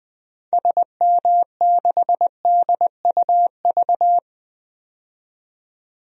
Morse code, heard as a keyed tone, about 21 wpm, 700 Hz. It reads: SM6DUV
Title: DXCC-100 Call Signs - 6 Characters - 20wpm